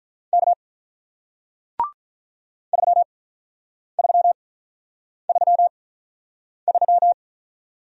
Morse code, heard as a keyed tone, about 48 wpm, spaced as for 9 wpm, 700 Hz. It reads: K 3333